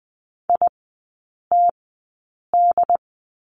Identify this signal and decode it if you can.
Morse code, heard as a keyed tone, about 20 wpm, spaced as for 8 wpm, 690 Hz. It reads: ITD
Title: 081 Lesson 14e - Number 2 - Three Characters 20wpm